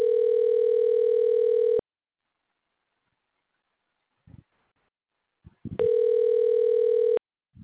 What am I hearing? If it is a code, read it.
Ringback tone (North America), 440 + 480 Hz.